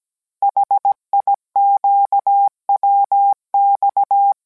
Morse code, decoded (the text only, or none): HIQWX